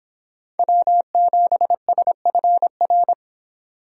Morse code, heard as a keyed tone, about 24 wpm, 700 Hz. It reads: W7SFR